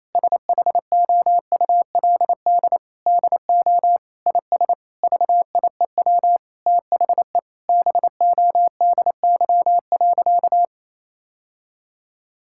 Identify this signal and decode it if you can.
Morse code, heard as a keyed tone, about 28 wpm, 700 Hz. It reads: SHOULD DO IS VIEW THE BODY.
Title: Sentences from Top 1000 Words - Review Part 6 - 28wpm